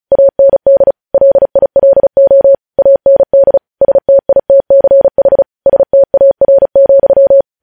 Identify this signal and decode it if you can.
Morse code, heard as a keyed tone, about 36 wpm, 550 Hz. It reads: AND LILO AND STITCH STAR,
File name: podcast / News headlines-1x-35x35